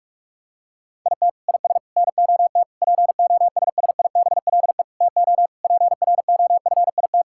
Morse code, heard as a keyed tone, about 46 wpm, 700 Hz. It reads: IT IS NOT POSSIBLE TO PROFIT
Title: Sentences from Top 1000 Words - Review Part 5 - 45wpm